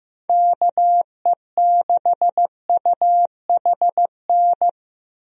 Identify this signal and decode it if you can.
Morse code, heard as a keyed tone, about 15 wpm, 700 Hz. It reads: KE6UHN
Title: USA Call Signs - 6 Characters - Repeated 3x - 15wpm